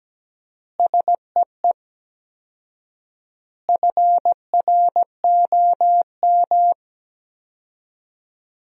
Morse code, heard as a keyed tone, about 17 wpm, 700 Hz. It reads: SEE FROM
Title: 4x Word-Spacing - Sentences from Top 100 Words - 17wpm